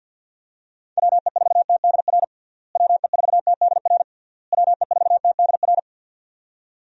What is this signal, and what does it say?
Morse code, heard as a keyed tone, about 50 wpm, 700 Hz. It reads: WE4TDR WE4TDR WE4TDR
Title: USA Call Signs - 6 Characters - Repeated 3x - 50wpm